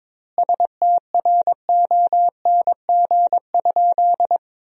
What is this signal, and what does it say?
Morse code, heard as a keyed tone, about 22 wpm, 700 Hz. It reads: STRONG?
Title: 2x Word-Spacing - Sentences from Top 500 Words - 22wpm